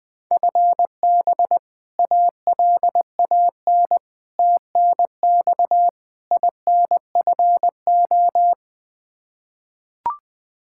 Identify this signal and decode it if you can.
Morse code, heard as a keyed tone, about 20 wpm, 700 Hz. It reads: FB ALAN TNX INFO
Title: Practice QSO - Exchange 3 - More Info - 20wpm